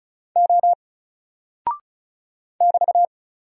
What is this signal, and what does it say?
Morse code, heard as a keyed tone, about 36 wpm, 700 Hz. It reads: O <BT>
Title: Single Character (Letter:Number:Punctuation:Prosign) Alt 35wpm